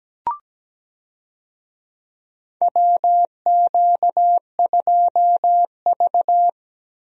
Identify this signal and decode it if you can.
Morse code, heard as a keyed tone, about 17 wpm, 700 Hz. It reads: WQ2V
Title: USA Call Signs - 4 Characters - Repeated 2x - 17wpm